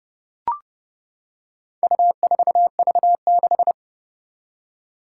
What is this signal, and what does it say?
Morse code, heard as a keyed tone, about 31 wpm, 700 Hz. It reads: U4V6
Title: Four Challenging Letter-Number Combinations 30wpm